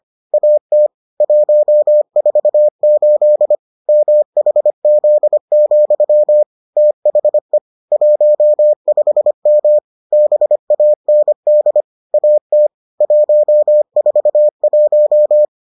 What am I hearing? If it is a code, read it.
Morse code, heard as a keyed tone, about 25 wpm, 600 Hz. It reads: AT 148 MHZ, THE 15M BAND AT 141